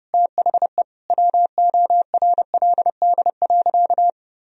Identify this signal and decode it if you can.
Morse code, heard as a keyed tone, about 30 wpm, 700 Hz. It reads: THE WORLD.
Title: Sentences from Top 1000 Words - Review Part 2 - 30wpm